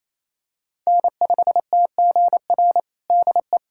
Morse code, heard as a keyed tone, about 28 wpm, 700 Hz. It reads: N5TGR DE
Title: USA Callsign DE Callsign 28wpm